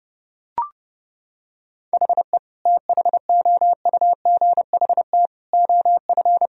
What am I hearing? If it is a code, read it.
Morse code, heard as a keyed tone, about 32 wpm, 700 Hz. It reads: HE THOUGHT OF
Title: Sentences from Top 1000 Words - Review Part 4 - 30wpm